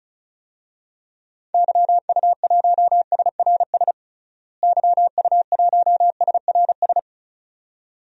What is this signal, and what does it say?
Morse code, heard as a keyed tone, about 34 wpm, 700 Hz. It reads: YU1SRS YU1SRS
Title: DXCC-100 Call Signs - 6 Characters - Repeated 2x - 35wpm